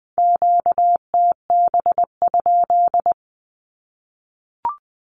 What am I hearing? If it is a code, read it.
Morse code, heard as a keyed tone, about 20 wpm, 700 Hz. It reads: QTB?
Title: Q-Codes Repeated 3x 20wpm